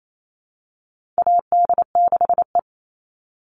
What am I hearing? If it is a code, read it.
Morse code, heard as a keyed tone, about 28 wpm, 700 Hz. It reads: AD6E
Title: Call Signs Repeated 2x - 28wpm